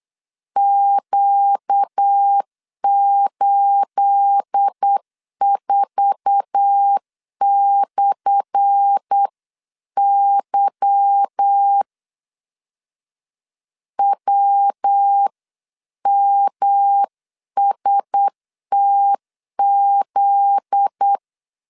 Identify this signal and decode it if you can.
Morse code, heard as a keyed tone, about 8 wpm, 800 Hz. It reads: Q84/Y WMSTZ
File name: LEZIONI / 10 SIGNS / 10B INT 8wpm 3534